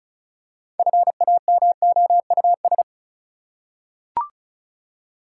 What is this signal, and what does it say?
Morse code, heard as a keyed tone, about 36 wpm, 700 Hz. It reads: FAMOUS
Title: Six Letter Words - 35wpm